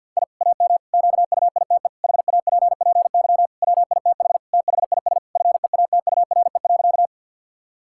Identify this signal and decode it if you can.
Morse code, heard as a keyed tone, about 52 wpm, 700 Hz. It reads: I AM QUITE HAPPY WITH THIS FEATURE.
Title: Sentences from Top 1000 Words - Review Part 5 - 50wpm